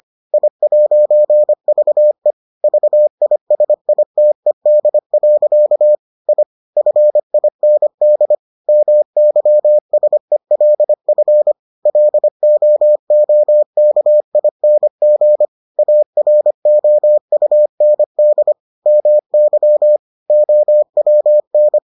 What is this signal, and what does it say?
Morse code, heard as a keyed tone, about 25 wpm, 600 Hz. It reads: I'VE VISITED. I FIND MYSELF LOOKING AROUND MY OWN